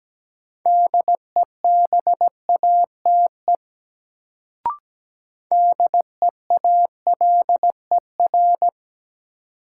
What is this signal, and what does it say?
Morse code, heard as a keyed tone, about 17 wpm, 700 Hz. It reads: DEBATE DEALER
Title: Sentences from Top 1100 Words - New Word Drill - 17wpm